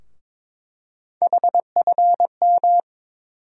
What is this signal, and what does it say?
Morse code, heard as a keyed tone, about 21 wpm, 700 Hz. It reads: HFM
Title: USA Call Sign Postfixes - 3 Letters 22wpm